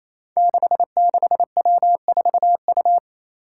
Morse code, heard as a keyed tone, about 28 wpm, 700 Hz. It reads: 66W4U
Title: Five Challenging Letter-Number Combinations 28wpm